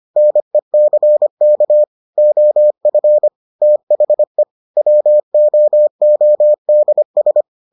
Morse code, heard as a keyed tone, about 25 wpm, 600 Hz. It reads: NECK OF THE WOODS